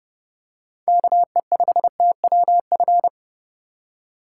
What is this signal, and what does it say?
Morse code, heard as a keyed tone, about 30 wpm, 700 Hz. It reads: KE5TWF
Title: USA Call Signs - 6 Characters - 2x Character Spacing - 30wpm